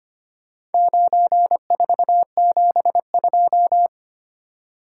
Morse code, heard as a keyed tone, about 25 wpm, 690 Hz. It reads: 9472
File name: practice / 25wpm / Four Letter-Number Combinations - 25wpm